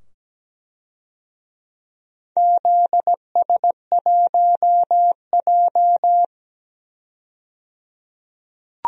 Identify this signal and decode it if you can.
Morse code, heard as a keyed tone, about 17 wpm, 700 Hz. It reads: ZS1J
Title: DXCC-100 Call Signs - 4 Characters - 17wpm